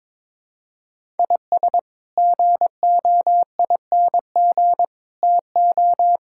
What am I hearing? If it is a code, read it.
Morse code, heard as a keyed tone, about 22 wpm, 700 Hz. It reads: IS GOING TO
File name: practice / 22wpm / Build A Phrase - 5-Word Phrases - 22wpm